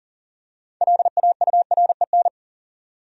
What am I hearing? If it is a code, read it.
Morse code, heard as a keyed tone, about 40 wpm, 700 Hz. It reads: LAUREN
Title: USA Common Names - 40wpm